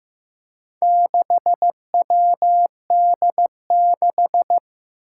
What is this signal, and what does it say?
Morse code, heard as a keyed tone, about 15 wpm, 700 Hz. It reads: 6WD6
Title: Four Challenging Letter-Number Combinations 15wpm